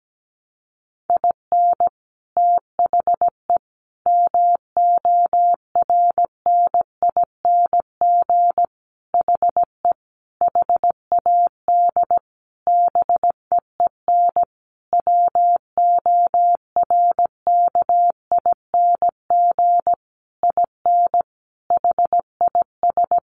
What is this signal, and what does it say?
Morse code, heard as a keyed tone, about 17 wpm, 700 Hz. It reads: IN THE MORNING HE HAD BEEN WORKING IN HIS
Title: Sentences from Top 1200 Words 17wpm